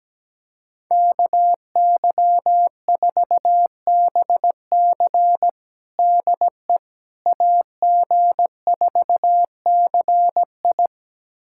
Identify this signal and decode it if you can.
Morse code, heard as a keyed tone, about 17 wpm, 700 Hz. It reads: KY4BC DE AG4CI
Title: USA Callsign DE Callsign - 5 Characters - 17wpm